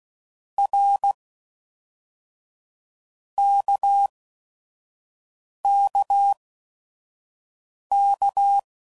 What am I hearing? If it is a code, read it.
Morse code, heard as a keyed tone, about 16 wpm, 800 Hz. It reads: R K K K
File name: Basic Letter Groups By7Groups / RKPX / RKPX@5 f15